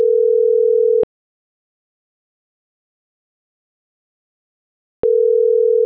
Ringback tone (North America), 440 + 480 Hz.